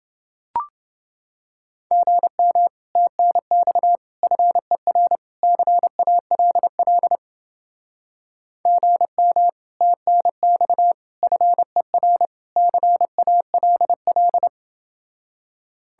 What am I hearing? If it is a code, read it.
Morse code, heard as a keyed tone, about 29 wpm, 700 Hz. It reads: GM TNX FER CALL GM TNX FER CALL